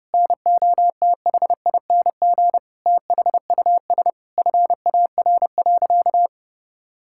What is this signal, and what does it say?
Morse code, heard as a keyed tone, about 30 wpm, 700 Hz. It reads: NOTHING THUS FAR.
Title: Sentences from Top 1000 Words - Review Part 5 - 30wpm